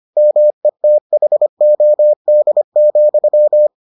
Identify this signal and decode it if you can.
Morse code, heard as a keyed tone, about 25 wpm, 600 Hz. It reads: METHOD,